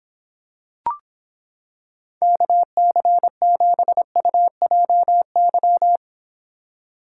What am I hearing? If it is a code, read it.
Morse code, heard as a keyed tone, about 26 wpm, 700 Hz. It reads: KC7UJY